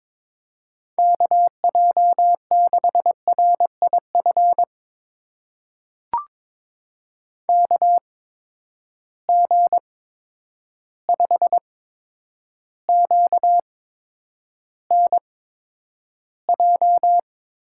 Morse code, heard as a keyed tone, about 22 wpm, spaced as for 9 wpm, 700 Hz. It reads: KJ6RIF KG5QNJ
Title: USA Call Signs - 8x Character Spacing - 22wpm